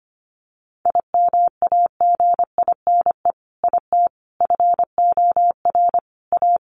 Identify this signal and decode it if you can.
Morse code, heard as a keyed tone, about 25 wpm, 700 Hz. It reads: IMAGINE IT FOR A
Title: Sentences from Top 1400 Words 25wpm